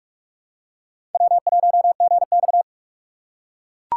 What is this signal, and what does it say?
Morse code, heard as a keyed tone, about 43 wpm, 700 Hz. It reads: W1GX
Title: USA Call Signs - 4 Characters - 8x Character Spacing - 45wpm